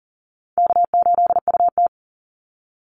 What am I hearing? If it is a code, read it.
Morse code, heard as a keyed tone, about 40 wpm, 700 Hz. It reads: K8UT
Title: Call Signs Repeated 3x - 40wpm